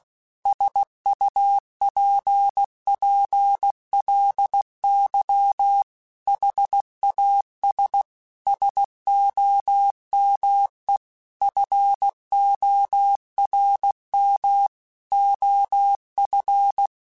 Morse code, heard as a keyed tone, about 16 wpm, 800 Hz. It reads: SUPPLY HAS SOME FORM OF
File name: Technical Subjects / Technical Subjects-mp3s / Sw Pwr Sup @15